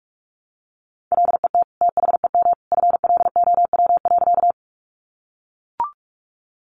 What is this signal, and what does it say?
Morse code, heard as a keyed tone, about 44 wpm, 700 Hz. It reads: LET THEM FLOW.
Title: Sentences from Top 1200 Words 45wpm